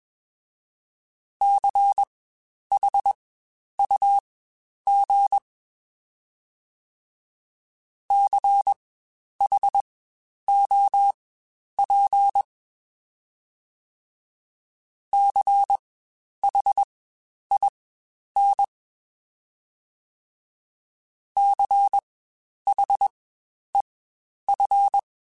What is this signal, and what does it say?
Morse code, heard as a keyed tone, about 21 wpm, spaced as for 10 wpm, 800 Hz. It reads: CHUG CHOP CHIN CHEF
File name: Spaced Alphas / Spaced Alphas mp3s / Cs sp @20